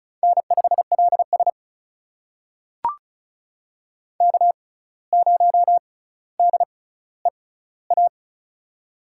Morse code, heard as a keyed tone, about 36 wpm, spaced as for 17 wpm, 700 Hz. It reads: N5LS K0DEA
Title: USA Call Signs - 6x Character Spacing - 35wpm